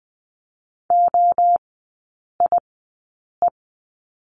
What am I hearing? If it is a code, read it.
Morse code, heard as a keyed tone, about 20 wpm, spaced as for 8 wpm, 700 Hz. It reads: OIE
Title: 041 Lesson 8e - Number 1 - Three Characters 20wpm